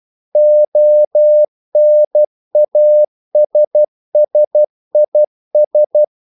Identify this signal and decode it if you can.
Morse code, heard as a keyed tone, about 12 wpm, 600 Hz. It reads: ONASSIS
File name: CW / Practice / EnglishQuotations0312